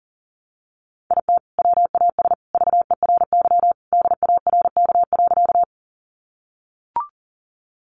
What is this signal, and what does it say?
Morse code, heard as a keyed tone, about 40 wpm, 700 Hz. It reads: IT WAS VERY DARK.
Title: Sentences from Top 500 Words 40wpm